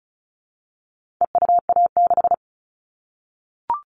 Morse code, heard as a keyed tone, about 34 wpm, 700 Hz. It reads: EUA6
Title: All Cut Numbers - 4-Digit 35wpm